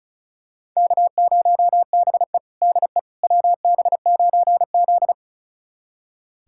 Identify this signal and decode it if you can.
Morse code, heard as a keyed tone, about 36 wpm, 700 Hz. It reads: K0BE DE WB9Z
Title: USA Callsign DE Callsign - 4 Characters - 35wpm